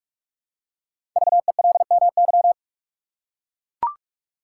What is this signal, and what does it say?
Morse code, heard as a keyed tone, about 44 wpm, 700 Hz. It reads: VELMY
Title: Code Groups - 5 Letters - 45wpm